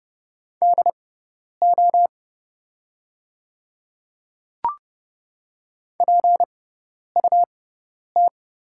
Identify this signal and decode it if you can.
Morse code, heard as a keyed tone, about 30 wpm, spaced as for 10 wpm, 690 Hz. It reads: DO PUT
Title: 6x Character Spacing - Top 200 Words - 30wpm